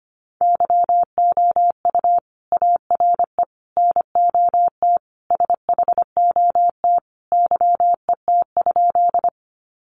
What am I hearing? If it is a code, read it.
Morse code, heard as a keyed tone, about 25 wpm, 700 Hz. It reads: YOU ARE NOT SHOT YET?
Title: Sentences from Top 1100 Words 25wpm